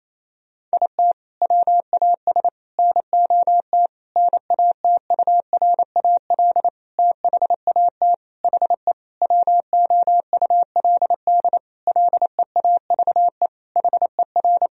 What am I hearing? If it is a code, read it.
Morse code, heard as a keyed tone, about 28 wpm, 700 Hz. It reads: IT WAS NOT NATURAL THAT HE WOULD LEAVE HER
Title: Sentences from Top 1000 Words - Review Part 4 - 28wpm